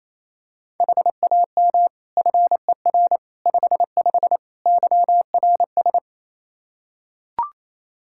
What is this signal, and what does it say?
Morse code, heard as a keyed tone, about 28 wpm, 700 Hz. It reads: HAM FER 55 YRS
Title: Practice QSO - Exchange 3 - More Info - 28wpm